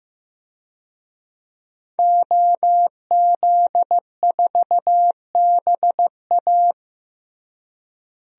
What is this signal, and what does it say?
Morse code, heard as a keyed tone, about 16 wpm, 700 Hz. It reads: OZ4BA
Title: DXCC-100 Call Signs - 5 Characters - Repeated 3x - 15wpm